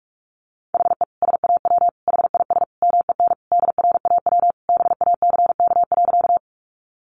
Morse code, heard as a keyed tone, about 44 wpm, 700 Hz. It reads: HE SAW HIS MEN DRAW BACK.
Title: Sentences from Top 300 Words 45wpm